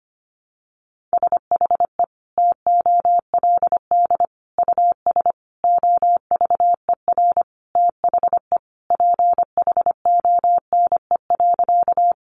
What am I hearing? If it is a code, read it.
Morse code, heard as a keyed tone, about 25 wpm, 700 Hz. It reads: SHE TOLD US OVER THE PHONE.
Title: Sentences from Top 1100 Words 25wpm